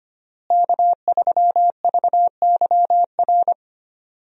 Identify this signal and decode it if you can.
Morse code, heard as a keyed tone, about 25 wpm, 700 Hz. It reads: K3VYR
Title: USA Call Signs - 5 Characters - Repeated 3x - 25wpm